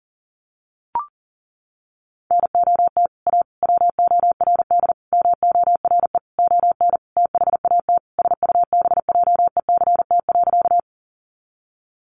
Morse code, heard as a keyed tone, about 40 wpm, 700 Hz. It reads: NOT A WORD MORE ON THAT SUBJECT.
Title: Sentences from Top 800 Words 40wpm